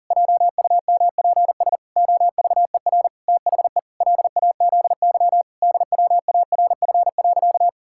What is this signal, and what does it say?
Morse code, heard as a keyed tone, about 40 wpm, 700 Hz. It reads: JUMPS OVER THE LAZY DWARF.
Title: Pangram Challenge v2 - 40wpm